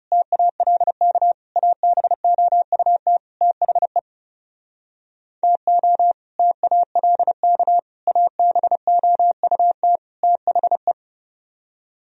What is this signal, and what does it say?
Morse code, heard as a keyed tone, about 40 wpm, 700 Hz. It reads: TALK ABOUT THE TO TALK ABOUT THE